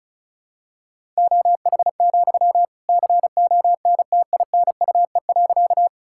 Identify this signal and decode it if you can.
Morse code, heard as a keyed tone, about 34 wpm, 700 Hz. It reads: OH, CONTINUE.
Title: Sentences from Top 1000 Words - Review Part 6 - 35wpm